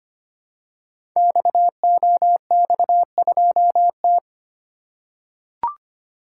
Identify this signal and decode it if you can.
Morse code, heard as a keyed tone, about 25 wpm, 700 Hz. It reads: XOX2T
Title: Code Groups - 5 Letter:Numbers - 25wpm